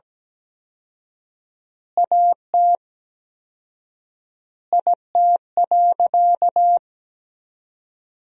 Morse code, heard as a keyed tone, about 17 wpm, 700 Hz. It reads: AT IT.
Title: 4x Word-Spacing - Sentences from Top 300 Words - 17wpm